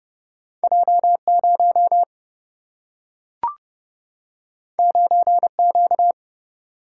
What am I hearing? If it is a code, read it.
Morse code, heard as a keyed tone, about 30 wpm, 700 Hz. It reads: J 0 9 Q
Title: Single-Letter Single-Number 30wpm